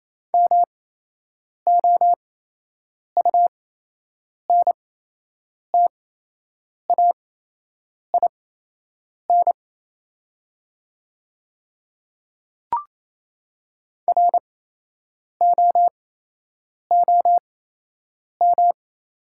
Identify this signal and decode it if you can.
Morse code, heard as a keyed tone, about 28 wpm, spaced as for 8 wpm, 700 Hz. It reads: MOUNTAIN ROOM
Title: 8x Character Spacing - Top 300 Words - 28wpm